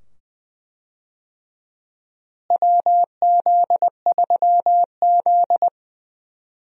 Morse code, heard as a keyed tone, about 20 wpm, 700 Hz. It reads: WZ3Z
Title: USA Call Signs - 4 Characters - Repeated 2x - 20wpm